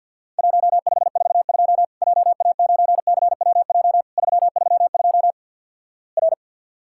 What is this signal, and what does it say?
Morse code, heard as a keyed tone, about 50 wpm, 700 Hz. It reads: 1542 JA9CWJ 332 R
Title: Worked All Europe DX Contest - QTC Exchange - 50wpm